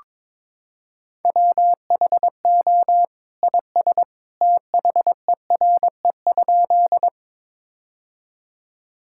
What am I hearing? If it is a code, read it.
Morse code, heard as a keyed tone, about 22 wpm, 700 Hz. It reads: WHO IS THERE?
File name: practice / 22wpm / Sentences with 2-Word Phrases - 22wpm